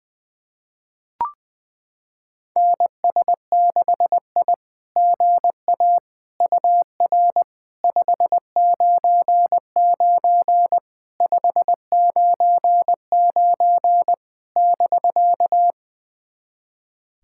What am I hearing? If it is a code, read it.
Morse code, heard as a keyed tone, about 20 wpm, 700 Hz. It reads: NS6I GA UR 599 599 <BK>
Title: SOTA QSO Practice 20wpm